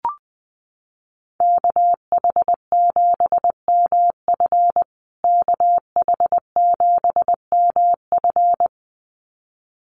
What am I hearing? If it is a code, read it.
Morse code, heard as a keyed tone, about 20 wpm, 700 Hz. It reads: KH7MF KH7MF
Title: USA Call Signs Repeated 2x 20wpm